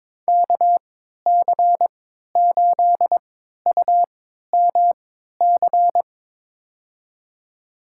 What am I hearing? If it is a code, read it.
Morse code, heard as a keyed tone, about 22 wpm, spaced as for 13 wpm, 700 Hz. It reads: KC8UMC
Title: USA Call Signs - 6 Characters - 3x Character Spacing - 22wpm